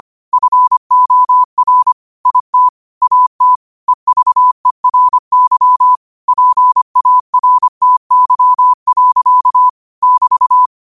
Morse code, heard as a keyed tone, about 25 wpm, 1000 Hz. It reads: FOR IT AT EVERY PARTY. <BT>
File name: qotdfiles / 25wpm